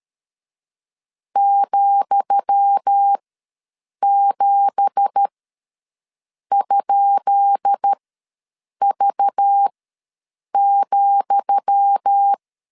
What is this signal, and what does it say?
Morse code, heard as a keyed tone, about 13 wpm, spaced as for 7 wpm, 800 Hz. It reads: ,7?V,
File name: LEZIONI / 10 SIGNS / 10D COMA ws 12wpm 3618